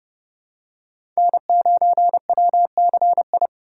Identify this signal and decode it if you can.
Morse code, heard as a keyed tone, about 31 wpm, 700 Hz. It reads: N9WCI
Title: USA Call Signs - 5 Characters - 2x Character Spacing - 30wpm